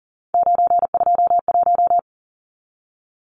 Morse code, heard as a keyed tone, about 40 wpm, 700 Hz. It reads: 921
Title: Numbers - 3-Digit 40wpm